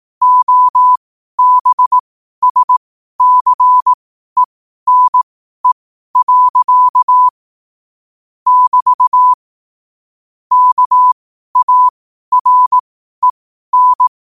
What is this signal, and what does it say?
Morse code, heard as a keyed tone, about 18 wpm, spaced as for 12 wpm, 1000 Hz. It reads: OBSCENE. <BT> KAREN
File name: qotdfiles / 12wpm